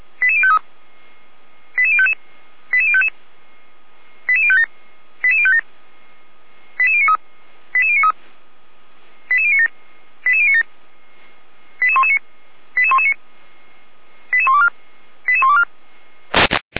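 Five-tone selective call (ZVEI), heard as quick runs of five tones, tones of about 70 ms. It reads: 88053 88055 88055 88057 88057 88093 88093 88097 88097 88119 88119 88124 88124
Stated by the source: ZVEI five-tone selective call 88019 received via SDR, logged on the Feuerwehr Orlamünde fire brigade audio server.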